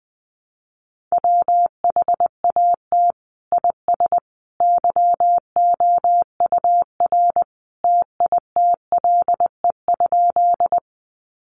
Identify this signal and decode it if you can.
Morse code, heard as a keyed tone, about 20 wpm, 700 Hz. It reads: WHAT IS YOUR TITLE?
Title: Sentences from Top 1200 Words 20wpm